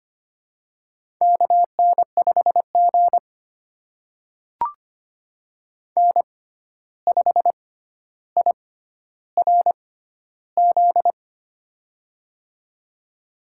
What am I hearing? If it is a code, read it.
Morse code, heard as a keyed tone, about 25 wpm, spaced as for 12 wpm, 700 Hz. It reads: KN5G N5IRZ
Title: USA Call Signs - 6x Character Spacing - 25wpm